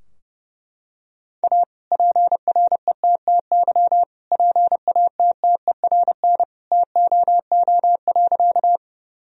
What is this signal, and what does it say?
Morse code, heard as a keyed tone, about 30 wpm, 700 Hz. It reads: A PRETTY PATTERN TOO.
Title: Sentences from Top 1000 Words - Review Part 2 - 30wpm